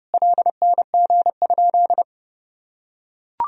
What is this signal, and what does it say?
Morse code, heard as a keyed tone, about 30 wpm, 700 Hz. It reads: LNG?
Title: Sentences from Top 200 Words and Abbreviations 30wpm